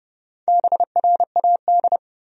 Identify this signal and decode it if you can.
Morse code, heard as a keyed tone, about 32 wpm, 700 Hz. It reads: BRAD
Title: USA Common Names - Max 5 Letters - 30wpm